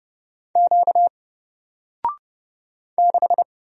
Morse code, heard as a keyed tone, about 30 wpm, 700 Hz. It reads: Q 6
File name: Single-Letter-Number-30wpm